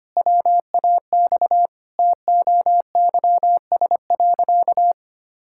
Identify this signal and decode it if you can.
Morse code, heard as a keyed tone, about 25 wpm, 700 Hz. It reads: WAX TOYS.
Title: Pangram Challenge v2 - 25wpm